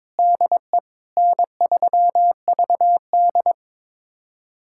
Morse code, heard as a keyed tone, about 21 wpm, 700 Hz. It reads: DE N3VD
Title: USA Callsign DE Callsign - 4 Characters - 22wpm